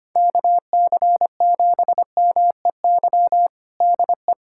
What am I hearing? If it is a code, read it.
Morse code, heard as a keyed tone, about 25 wpm, 700 Hz. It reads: KC7MEY DE
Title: USA Callsign DE Callsign 25wpm